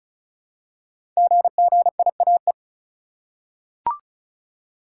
Morse code, heard as a keyed tone, about 36 wpm, 700 Hz. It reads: GGIAE
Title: Code Groups - 5 Letters - 35wpm